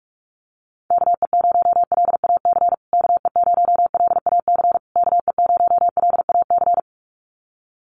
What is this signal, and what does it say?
Morse code, heard as a keyed tone, about 44 wpm, 690 Hz. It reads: KE0LAC KE0LAC KE0LAC
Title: USA Call Signs Repeated 3x 45wpm